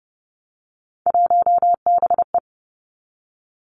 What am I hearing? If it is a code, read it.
Morse code, heard as a keyed tone, about 30 wpm, 700 Hz. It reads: 1BE
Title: Three Challenging Letter-Number Combinations 30wpm